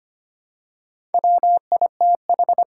Morse code, heard as a keyed tone, about 25 wpm, 700 Hz. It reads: WITH
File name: practice / 25wpm / All-Age Appropriate Jokes - 25wpm